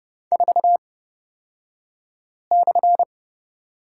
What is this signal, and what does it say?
Morse code, heard as a keyed tone, about 30 wpm, spaced as for 5 wpm, 700 Hz. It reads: 4/
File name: Lesson 38d - Letter q - Two Characters 30wpm